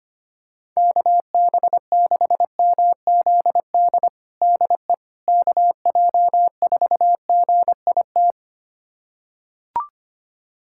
Morse code, heard as a keyed tone, about 25 wpm, 700 Hz. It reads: KB6MZD DE KJ4GIT
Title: USA Callsign DE Callsign - 6 Characters - 25wpm